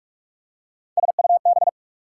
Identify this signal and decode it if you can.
Morse code, heard as a keyed tone, about 44 wpm, 690 Hz. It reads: SUB
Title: Prefixes and Suffixes 45wpm